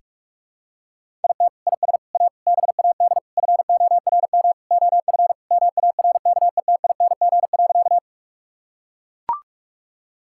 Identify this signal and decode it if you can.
Morse code, heard as a keyed tone, about 42 wpm, 700 Hz. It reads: IT IS A BAD FORM OF MARKETING.
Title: Sentences from Top 1000 Words - Review Part 5 - 45wpm